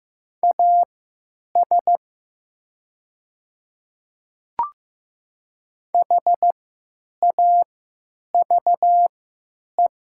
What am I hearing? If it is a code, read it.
Morse code, heard as a keyed tone, about 15 wpm, spaced as for 8 wpm, 700 Hz. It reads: AS HAVE
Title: 3x Character Spacing - Top 100 Words - 15wpm